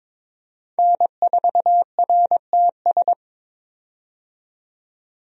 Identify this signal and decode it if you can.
Morse code, heard as a keyed tone, about 22 wpm, 700 Hz. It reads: N4RTS
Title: USA Call Signs - 5 Characters - 22wpm